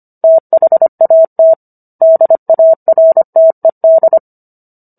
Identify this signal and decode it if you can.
Morse code, heard as a keyed tone, about 25 wpm, 650 Hz. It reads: THAT DARTED